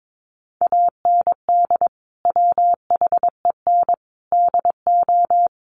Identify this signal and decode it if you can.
Morse code, heard as a keyed tone, about 22 wpm, 700 Hz. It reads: AND WHEN DO
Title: Sentences from Top 1100 Words 22wpm